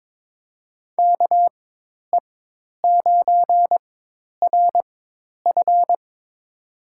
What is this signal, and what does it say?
Morse code, heard as a keyed tone, about 22 wpm, spaced as for 10 wpm, 700 Hz. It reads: KE9RF
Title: USA Call Signs - 5 Characters - 4x Character Spacing - 22wpm